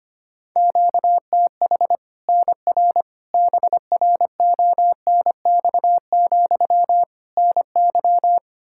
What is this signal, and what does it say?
Morse code, heard as a keyed tone, about 25 wpm, 700 Hz. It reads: QTH NR BRONX, NY
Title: Practice QSO - Exchange 1 - Beginning - 25wpm